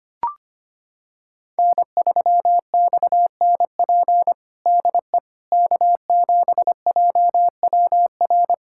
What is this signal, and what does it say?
Morse code, heard as a keyed tone, about 25 wpm, 700 Hz. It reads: N3XNP DE K7JWR
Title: USA Callsign DE Callsign - 5 Characters - 25wpm